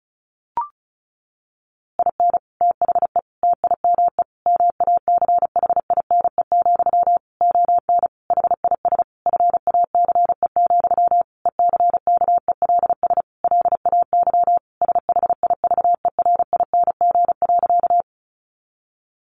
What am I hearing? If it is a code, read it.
Morse code, heard as a keyed tone, about 34 wpm, 700 Hz. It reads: IN THE TIME MACHINE, ON HIS FACE, ECKELS LAY SHIVERING.
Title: A Sound Of Thunder 35wpm